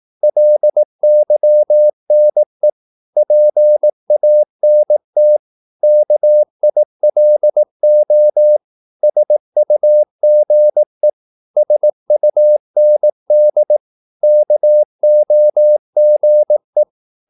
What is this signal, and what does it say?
Morse code, heard as a keyed tone, about 18 wpm, 600 Hz. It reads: LYNE PANT KILO SUGE SUND KOGE
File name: Korte ord DK 18wpm